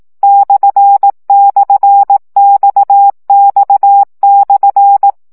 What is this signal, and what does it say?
Morse code, heard as a keyed tone, about 18 wpm, 800 Hz. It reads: //XX/
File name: CW gyakorlas x per 15 baud